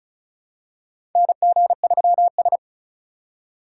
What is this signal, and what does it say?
Morse code, heard as a keyed tone, about 36 wpm, 700 Hz. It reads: NG3S
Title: USA Call Signs - 4 Characters - 3x Character Spacing - 35wpm